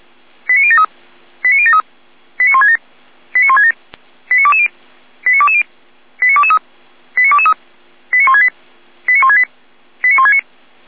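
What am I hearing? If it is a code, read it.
Five-tone selective call (ZVEI), heard as quick runs of five tones, tones of about 70 ms. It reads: 89073 89073 89167 89167 89220 89220 89223 89223 89168 89168 89169
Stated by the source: ZVEI five-tone call received on an SDR.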